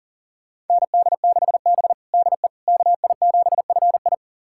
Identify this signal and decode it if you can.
Morse code, heard as a keyed tone, about 40 wpm, 700 Hz. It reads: ND6B DE KI7FI
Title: USA Callsign DE Callsign 40wpm